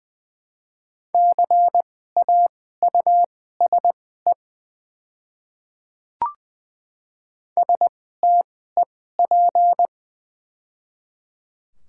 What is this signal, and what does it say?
Morse code, heard as a keyed tone, about 20 wpm, spaced as for 15 wpm, 700 Hz. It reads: CAUSE STEP
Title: 2x Character Spacing - Top 500 Words - 20wpm